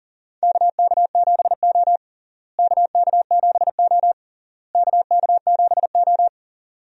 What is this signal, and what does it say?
Morse code, heard as a keyed tone, about 40 wpm, 700 Hz. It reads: KK7O KK7O KK7O
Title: USA Call Signs - 4 Characters - Repeated 3x - 40wpm